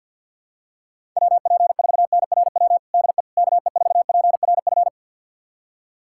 Morse code, heard as a keyed tone, about 50 wpm, 700 Hz. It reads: WP4NRW DE KE4PRF
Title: USA Callsign DE Callsign - 6 Characters - 50wpm